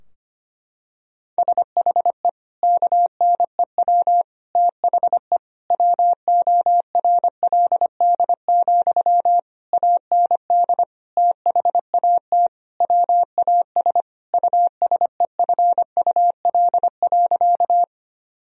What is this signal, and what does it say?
Morse code, heard as a keyed tone, about 25 wpm, 690 Hz. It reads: SHE KNEW THE WORLD, AND THAT WAS USEFUL.
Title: Sentences from Top 1000 Words - Review Part 4 - 25wpm